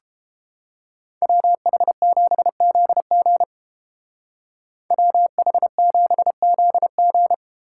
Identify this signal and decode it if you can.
Morse code, heard as a keyed tone, about 31 wpm, 700 Hz. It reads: WH7ZG WH7ZG